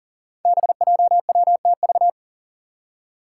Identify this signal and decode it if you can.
Morse code, heard as a keyed tone, about 40 wpm, 690 Hz. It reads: BJWTV
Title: Five Challenging Letter-Number Combinations 40wpm